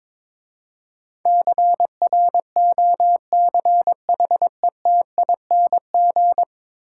Morse code, heard as a keyed tone, about 25 wpm, 700 Hz. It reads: CROCHETING
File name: Easy Words - Speed-Racing with Hints - 22wpm